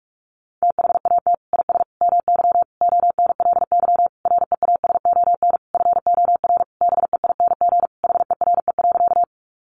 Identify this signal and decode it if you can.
Morse code, heard as a keyed tone, about 44 wpm, 700 Hz. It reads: THAT IS MY ONLY REASON FOR BEING HERE.
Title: Sentences from Top 700 Words 45wpm